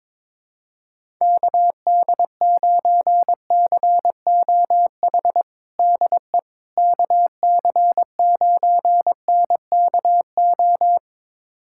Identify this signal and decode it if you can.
Morse code, heard as a keyed tone, about 22 wpm, 700 Hz. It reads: KD9COH DE KC9NKO
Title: USA Callsign DE Callsign - 6 Characters - 22wpm